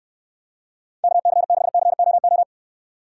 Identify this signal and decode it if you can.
Morse code, heard as a keyed tone, about 69 wpm, 700 Hz. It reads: KC6COO